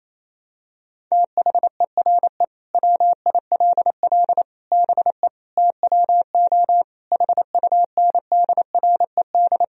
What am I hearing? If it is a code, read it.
Morse code, heard as a keyed tone, about 28 wpm, 700 Hz. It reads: THERE WILL BE TWO HUNDRED
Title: Sentences from Top 1000 Words - Review Part 3 - 28wpm